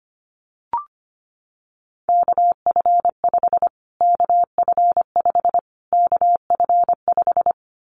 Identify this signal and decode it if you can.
Morse code, heard as a keyed tone, about 25 wpm, 700 Hz. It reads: KF5 KF5 KF5
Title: Callsign Prefixes for Top 25 DXCC - Repeated 3x - 25wpm